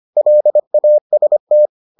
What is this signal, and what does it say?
Morse code, heard as a keyed tone, about 25 wpm, 600 Hz. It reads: LAST